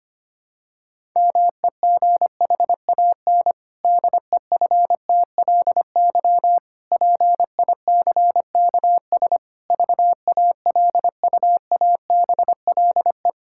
Morse code, heard as a keyed tone, about 25 wpm, 700 Hz. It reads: MEGHAN DEFTLY PICKS VALUABLE
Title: Pangram Challenge v2 - 25wpm